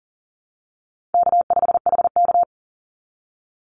Morse code, heard as a keyed tone, about 40 wpm, 700 Hz. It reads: K5HK
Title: Call Signs Repeated 2x - 40wpm